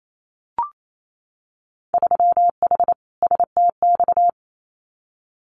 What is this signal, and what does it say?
Morse code, heard as a keyed tone, about 28 wpm, 700 Hz. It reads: 3H STX
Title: Winter Field Day Exchanges 28wpm